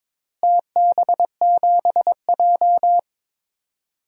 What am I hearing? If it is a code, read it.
Morse code, heard as a keyed tone, about 22 wpm, 700 Hz. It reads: TB7J
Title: DXCC-100 Call Signs - 4 Characters - Repeated 3x - 22wpm